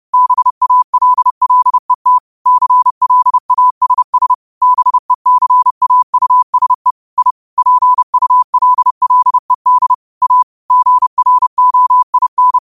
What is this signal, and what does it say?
Morse code, heard as a keyed tone, about 30 wpm, 1000 Hz. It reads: BALLET CLASS BECAUSE I PULLED A GROIN